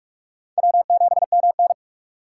Morse code, heard as a keyed tone, about 46 wpm, 700 Hz. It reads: W7MN
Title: USA Call Signs - 4 Characters - 4x Character Spacing - 45wpm